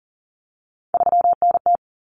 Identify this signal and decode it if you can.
Morse code, heard as a keyed tone, about 40 wpm, 700 Hz. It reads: 3NT
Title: Common Cut Numbers - 3-Digit 40wpm